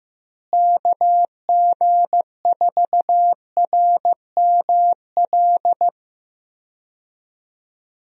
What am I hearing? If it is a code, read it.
Morse code, heard as a keyed tone, about 15 wpm, 700 Hz. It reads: KG4RML
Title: USA Call Signs - 6 Characters - Repeated 2x - 15wpm